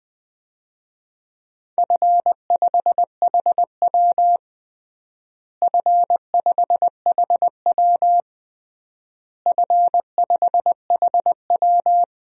Morse code, heard as a keyed tone, about 21 wpm, 700 Hz. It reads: F5HW F5HW F5HW
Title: DXCC-100 Call Signs - 4 Characters - Repeated 3x - 20wpm